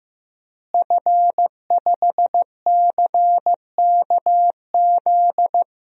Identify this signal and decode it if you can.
Morse code, heard as a keyed tone, about 15 wpm, 700 Hz. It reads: F5CKZ
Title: DXCC-100 Call Signs - 5 Characters - Repeated 2x - 15wpm